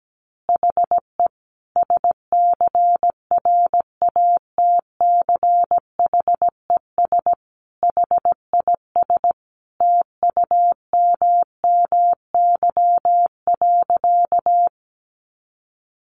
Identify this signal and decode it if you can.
Morse code, heard as a keyed tone, about 17 wpm, 700 Hz. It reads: HE SCRATCHES HIS TUMMY.
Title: Easy Phrases 17wpm